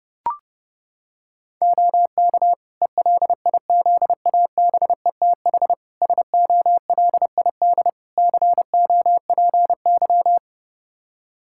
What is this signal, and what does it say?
Morse code, heard as a keyed tone, about 31 wpm, 700 Hz. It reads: OK ELIZABETH SOLID COPY
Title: Practice QSO - Exchange 3 - More Info - 30wpm